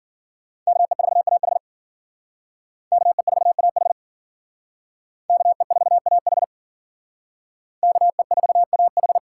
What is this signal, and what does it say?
Morse code, heard as a keyed tone, about 52 wpm, 700 Hz. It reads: KE4AH KE4AH KE4AH KE4AH